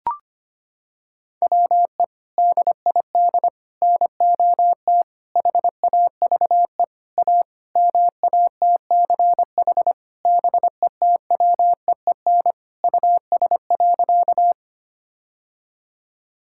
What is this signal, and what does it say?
Morse code, heard as a keyed tone, about 25 wpm, 700 Hz. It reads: WE DID NOT HAVE A MATCH BETWEEN US.
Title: Sentences from Top 1600 Words 25wpm